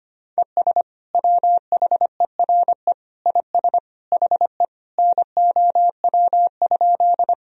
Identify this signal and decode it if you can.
Morse code, heard as a keyed tone, about 25 wpm, 700 Hz. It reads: ES WHERE IS HE NOW?
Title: Sentences from Top 200 Words and Abbreviations 25wpm